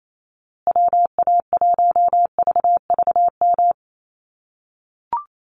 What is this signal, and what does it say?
Morse code, heard as a keyed tone, about 28 wpm, 700 Hz. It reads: WA1VVM
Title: USA Call Signs Repeated 2x 28wpm